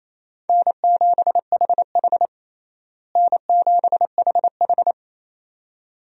Morse code, heard as a keyed tone, about 28 wpm, 700 Hz. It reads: N7HH N7HH
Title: USA Call Signs - 4 Characters - Repeated 2x - 28wpm